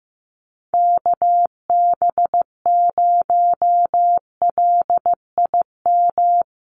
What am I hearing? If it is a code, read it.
Morse code, heard as a keyed tone, about 15 wpm, 700 Hz. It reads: KB0LIM
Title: USA Call Signs 15wpm